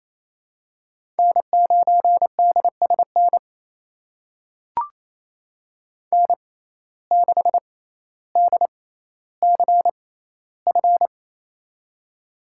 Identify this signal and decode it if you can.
Morse code, heard as a keyed tone, about 28 wpm, spaced as for 14 wpm, 700 Hz. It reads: N9DSN N6DCF
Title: USA Call Signs - 5 Characters - 6x Character Spacing - 28wpm